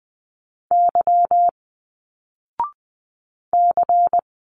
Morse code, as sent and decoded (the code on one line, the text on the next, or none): -.-- / -.-.
Y C